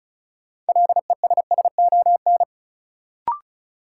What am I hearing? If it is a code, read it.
Morse code, heard as a keyed tone, about 36 wpm, 700 Hz. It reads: LESSON
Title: Sentences from Top 1200 Words - New Word Drill Alt - 35wpm